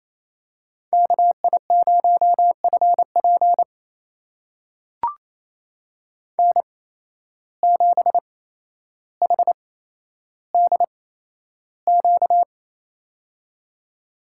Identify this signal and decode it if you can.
Morse code, heard as a keyed tone, about 28 wpm, spaced as for 12 wpm, 700 Hz. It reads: KI0FP N7HDQ
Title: USA Call Signs - 8x Character Spacing - 28wpm